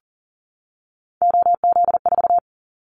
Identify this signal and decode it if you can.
Morse code, heard as a keyed tone, about 38 wpm, 700 Hz. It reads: OZ4
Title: Callsign Prefixes for Top 100 DXCC - 40wpm